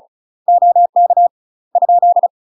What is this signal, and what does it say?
Morse code, heard as a keyed tone, about 33 wpm, 700 Hz. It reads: OK ?
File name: UK MASTER QSOs TRAINING / 35WPM / UK MASTER QSOs 0029